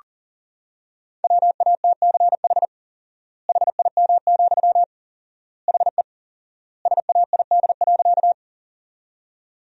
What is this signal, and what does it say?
Morse code, heard as a keyed tone, about 40 wpm, 700 Hz. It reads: WATCH HIM, HE SAID.
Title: 4x Word-Spacing - Sentences from Top 300 Words - 40wpm